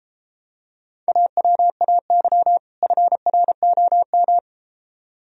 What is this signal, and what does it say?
Morse code, heard as a keyed tone, about 32 wpm, 700 Hz. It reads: AWAY FROM